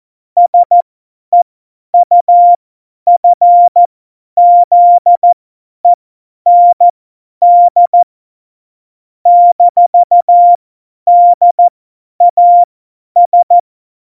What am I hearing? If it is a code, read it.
Morse code, heard as a keyed tone, about 14 wpm, spaced as for 10 wpm, 700 Hz. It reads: SEUFZEND -DAS
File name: Morsekurs / Witze / Witz 15. Mai 2018, 50 F70